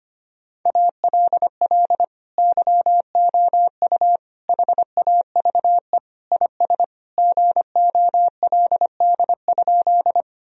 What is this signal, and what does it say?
Morse code, heard as a keyed tone, about 25 wpm, 700 Hz. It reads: ALL YOU HAVE IS GOLD?
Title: Sentences from Top 1000 Words - Review Part 3 - 25wpm